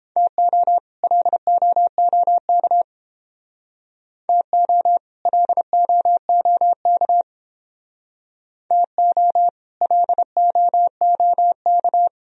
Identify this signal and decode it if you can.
Morse code, heard as a keyed tone, about 30 wpm, 700 Hz. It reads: TO LOOK TO LOOK TO LOOK